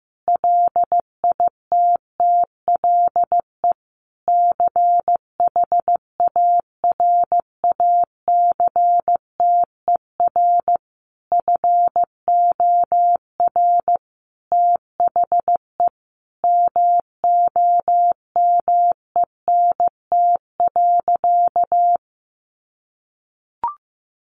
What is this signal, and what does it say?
Morse code, heard as a keyed tone, about 15 wpm, 700 Hz. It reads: LITTLE CHARACTER FOR THE MOMENT.
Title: Sentences from Top 1200 Words 15wpm